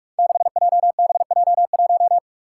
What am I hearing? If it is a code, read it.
Morse code, heard as a keyed tone, about 42 wpm, 700 Hz. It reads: 6JBJ1
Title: Five Challenging Letter-Number Combinations 45wpm